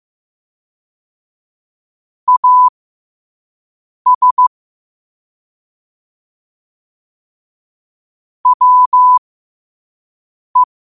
Morse code, heard as a keyed tone, about 15 wpm, spaced as for 5 wpm, 1000 Hz. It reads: AS WE
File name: qotdfiles / 5wpm